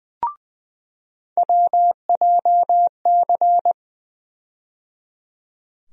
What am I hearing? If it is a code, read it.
Morse code, heard as a keyed tone, about 20 wpm, 700 Hz. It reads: WJC
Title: USA Call Sign Postfixes - 3 Letters 20wpm